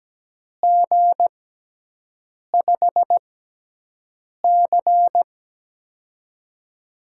Morse code, heard as a keyed tone, about 17 wpm, spaced as for 6 wpm, 700 Hz. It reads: G5C
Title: USA Call Signs - 4 Characters - 6x Character Spacing - 17wpm